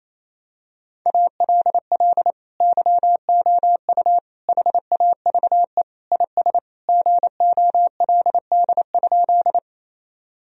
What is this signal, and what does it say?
Morse code, heard as a keyed tone, about 28 wpm, 700 Hz. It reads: ALL YOU HAVE IS GOLD?
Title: Sentences from Top 1000 Words - Review Part 3 - 28wpm